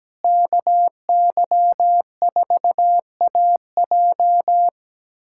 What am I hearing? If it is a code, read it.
Morse code, heard as a keyed tone, about 17 wpm, 700 Hz. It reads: KY4AJ
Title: USA Callsign DE Callsign - 5 Characters - 17wpm